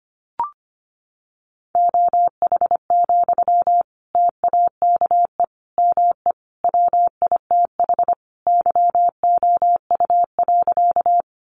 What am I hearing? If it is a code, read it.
Morse code, heard as a keyed tone, about 25 wpm, 700 Hz. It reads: OH, TAKE ME WITH YOU.
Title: Sentences from Top 500 Words 25wpm